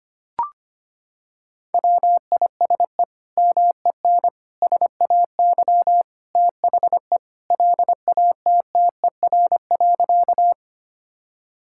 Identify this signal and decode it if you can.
Morse code, heard as a keyed tone, about 25 wpm, 700 Hz. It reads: WISE MEN SAY THE LATTER.
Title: Sentences from Top 1600 Words 25wpm